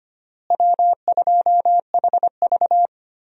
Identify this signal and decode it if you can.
Morse code, heard as a keyed tone, about 25 wpm, 700 Hz. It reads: W2HV
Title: USA Call Signs - 4 Characters - Repeated 3x - 25wpm